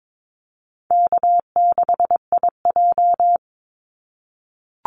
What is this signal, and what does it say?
Morse code, heard as a keyed tone, about 22 wpm, 700 Hz. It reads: K6IJ
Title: Easy Call Signs - 22wpm